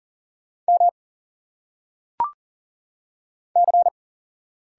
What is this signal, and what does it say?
Morse code, heard as a keyed tone, about 40 wpm, 700 Hz. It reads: M C
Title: Single Letters - Phonetic Pronunciation 40wpm